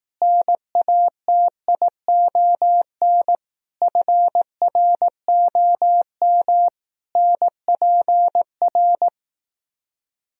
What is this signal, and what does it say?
Morse code, heard as a keyed tone, about 18 wpm, 700 Hz. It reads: NATION FROM NPR
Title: Speed-Racing - Sets of 5 Words 15wpm